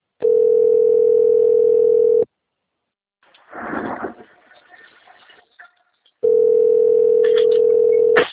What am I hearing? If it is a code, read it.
Ringback tone (North America), 440 + 480 Hz.
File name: rg-1600-unknown- - - .46720